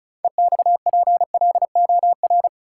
Morse code, heard as a keyed tone, about 34 wpm, 700 Hz. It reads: EXPLOR
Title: Sentences from Top 1100 Words - New Word Drill Alt - 35wpm